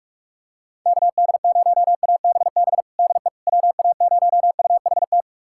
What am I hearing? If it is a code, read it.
Morse code, heard as a keyed tone, about 46 wpm, 700 Hz. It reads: KD0ABB DE WA0UHT
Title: USA Callsign DE Callsign - 6 Characters - 45wpm